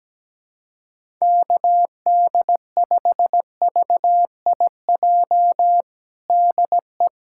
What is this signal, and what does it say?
Morse code, heard as a keyed tone, about 17 wpm, 700 Hz. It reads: KD5VIJ DE
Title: USA Callsign DE Callsign - 6 Characters - 17wpm